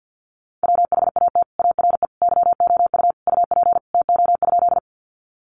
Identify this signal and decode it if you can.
Morse code, heard as a keyed tone, about 50 wpm, 700 Hz. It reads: WHAT ARE YOU UP TO?
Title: Sentences from Top 100 Words 50wpm